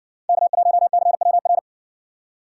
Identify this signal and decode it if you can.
Morse code, heard as a keyed tone, about 57 wpm, 700 Hz. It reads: K1CWR